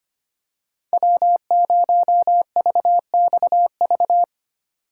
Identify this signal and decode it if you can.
Morse code, heard as a keyed tone, about 25 wpm, 700 Hz. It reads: W0VXV
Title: USA Call Signs - 5 Characters - 3x Character Spacing - 25wpm